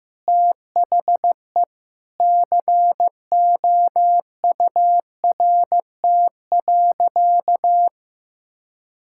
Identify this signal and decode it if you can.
Morse code, heard as a keyed tone, about 16 wpm, 700 Hz. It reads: THE COURT.
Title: Sentences from Top 1600 Words 15wpm